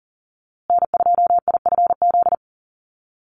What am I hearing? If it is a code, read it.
Morse code, heard as a keyed tone, about 40 wpm, 700 Hz. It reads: N2IFZ
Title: USA Call Signs Repeated 3x 40wpm